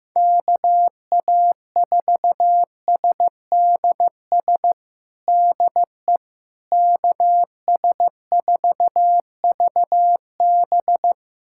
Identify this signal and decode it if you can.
Morse code, heard as a keyed tone, about 15 wpm, 700 Hz. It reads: KA4SDS DE KS4VB
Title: USA Callsign DE Callsign 15wpm